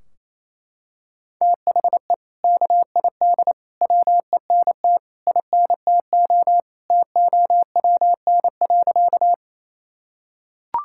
Morse code, heard as a keyed tone, about 28 wpm, 700 Hz. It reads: THE KID WENT INTO TOWN.
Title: Sentences from Top 1600 Words 28wpm